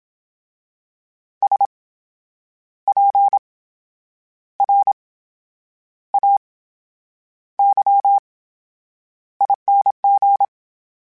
Morse code, heard as a keyed tone, about 27 wpm, spaced as for 8 wpm, 800 Hz. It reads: SPRAYING
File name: Special INGs / Special INGs-mp3s / VerySlowToFaster Ing @25